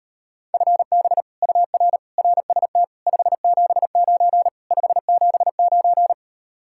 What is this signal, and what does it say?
Morse code, heard as a keyed tone, about 38 wpm, 700 Hz. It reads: FB UR RST 579 579